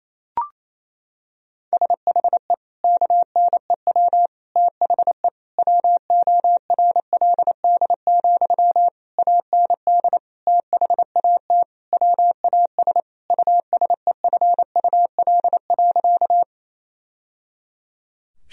Morse code, heard as a keyed tone, about 28 wpm, 700 Hz. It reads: SHE KNEW THE WORLD, AND THAT WAS USEFUL.
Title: Sentences from Top 1000 Words - Review Part 4 - 28wpm